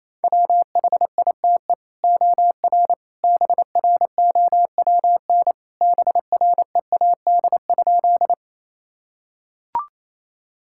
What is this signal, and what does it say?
Morse code, heard as a keyed tone, about 28 wpm, 700 Hz. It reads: WHITE OR BROWN BREAD?
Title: Sentences from Top 1600 Words 28wpm